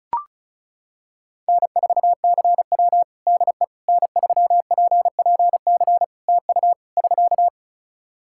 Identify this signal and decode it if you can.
Morse code, heard as a keyed tone, about 34 wpm, 700 Hz. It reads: N4CW DE N3PPC TU <SK>
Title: Practice QSO - Exchange 4 - Ending - 35wpm